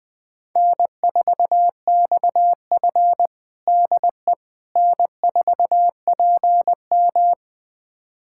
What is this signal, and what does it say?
Morse code, heard as a keyed tone, about 21 wpm, 700 Hz. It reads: N4XF DE N4PM
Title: USA Callsign DE Callsign - 4 Characters - 20wpm